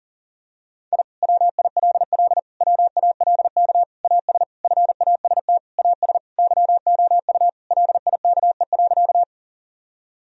Morse code, heard as a keyed tone, about 40 wpm, 700 Hz. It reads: I WILL WALK AS FAST AS YOU LIKE.
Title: Sentences from Top 1000 Words - Review Part 2 - 40wpm